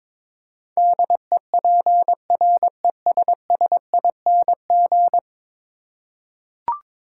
Morse code, heard as a keyed tone, about 22 wpm, 700 Hz. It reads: DEPRESSING
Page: morse-code-ninja-practice-22wpm